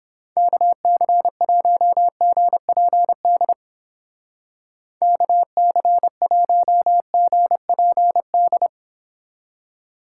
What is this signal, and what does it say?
Morse code, heard as a keyed tone, about 31 wpm, 700 Hz. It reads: KC1GPD KC1GPD